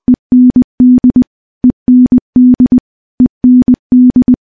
Morse code, heard as a keyed tone, about 20 wpm, 250 Hz. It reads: END END END
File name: morse-training-20WPM